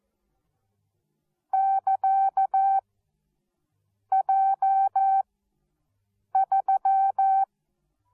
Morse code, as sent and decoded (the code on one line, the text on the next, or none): -.-.- / .--- / ...--
<KA> J 3